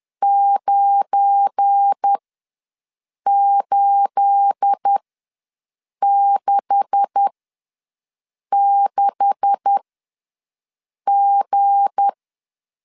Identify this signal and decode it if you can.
Morse code, heard as a keyed tone, about 10 wpm, spaced as for 5 wpm, 800 Hz. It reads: 9866G
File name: LEZIONI / 9 286 ws 10wpm 4032